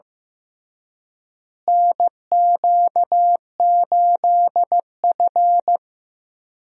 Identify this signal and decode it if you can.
Morse code, heard as a keyed tone, about 15 wpm, 700 Hz. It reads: NQ8F
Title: USA Call Signs - 4 Characters - Repeated 3x - 15wpm